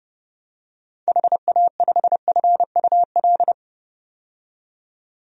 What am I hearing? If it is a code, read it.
Morse code, heard as a keyed tone, about 32 wpm, 700 Hz. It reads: HA5FUL
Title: DXCC-100 Call Signs - 6 Characters - 30wpm